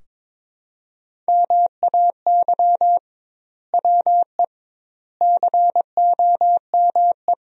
Morse code, heard as a keyed tone, about 22 wpm, 700 Hz. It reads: MAY WE COME
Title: 2x Word-Spacing - Sentences from Top 100 Words - 22wpm